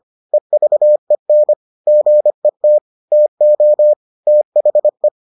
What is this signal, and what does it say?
Morse code, heard as a keyed tone, about 25 wpm, 600 Hz. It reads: EVEN GET TO THE